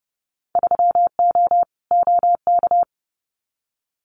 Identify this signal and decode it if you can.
Morse code, heard as a keyed tone, about 30 wpm, 700 Hz. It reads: 3O OK
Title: Winter Field Day Exchanges Encore 30wpm